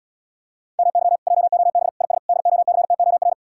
Morse code, heard as a keyed tone, about 80 wpm, 700 Hz. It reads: MY JOB IS DOCTOR